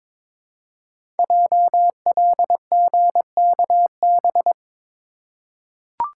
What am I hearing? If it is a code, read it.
Morse code, heard as a keyed tone, about 22 wpm, 700 Hz. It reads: JLGKB
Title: Code Groups - 5 Letters - 22wpm